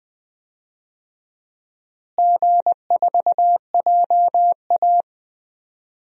Morse code, heard as a keyed tone, about 21 wpm, 700 Hz. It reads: G4JA
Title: DXCC-100 Call Signs - 4 Characters - Repeated 3x - 20wpm